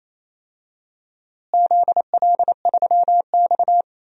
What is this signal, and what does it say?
Morse code, heard as a keyed tone, about 28 wpm, 700 Hz. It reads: ZL3X
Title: DXCC-100 Call Signs - 4 Characters - Repeated 2x - 28wpm